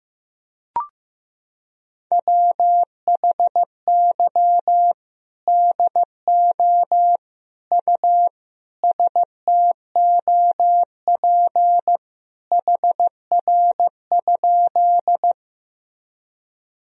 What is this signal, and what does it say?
Morse code, heard as a keyed tone, about 15 wpm, 700 Hz. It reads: WHY DO U STOP HR?
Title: Sentences from Top 300 Words and Abbreviations 15wpm